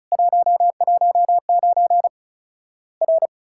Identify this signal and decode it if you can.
Morse code, heard as a keyed tone, about 36 wpm, 700 Hz. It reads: 119 R
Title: Worked All Europe DX Contest - QTC Exchange - 35wpm